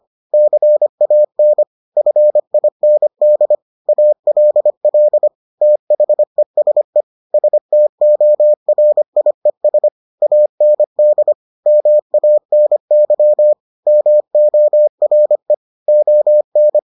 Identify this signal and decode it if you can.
Morse code, heard as a keyed tone, about 25 wpm, 600 Hz. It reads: CAN FIND ALL THESE STORIES AND MANY MORE ON